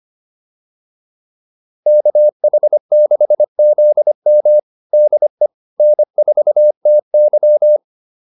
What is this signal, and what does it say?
Morse code, heard as a keyed tone, about 25 wpm, 600 Hz. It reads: KH6ZM DE N4TY